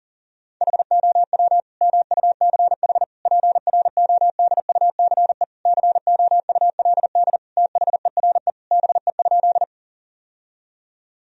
Morse code, heard as a keyed tone, about 40 wpm, 700 Hz. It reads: HOW MUCH PRODUCE COULD THERE BE?
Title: Sentences from Top 1000 Words - Review Part 3 - 40wpm